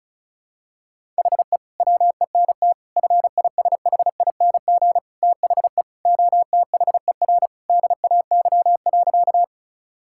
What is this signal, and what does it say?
Morse code, heard as a keyed tone, about 34 wpm, 700 Hz. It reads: HE WENT FISHING THE OTHER DAY.
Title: Sentences from Top 1000 Words - Review Part 5 - 35wpm